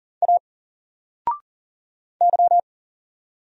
Morse code, heard as a keyed tone, about 40 wpm, 700 Hz. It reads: A Y
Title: Single Letter-Number 40wpm